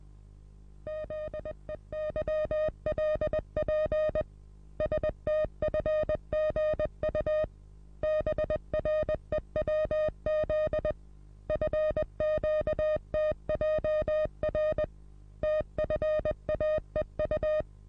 Morse code, heard as a keyed tone, about 20 wpm, 620 Hz. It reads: ZEYLP STFGU BREWZ FQTJR TFAEU
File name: repos / Morsekurs / DJ4UF Morsen27